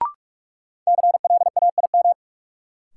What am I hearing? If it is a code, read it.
Morse code, heard as a keyed tone, about 47 wpm, 700 Hz. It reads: CLAIM
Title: Sentences from Top 1500 Words - New Word Drill Alt - 45wpm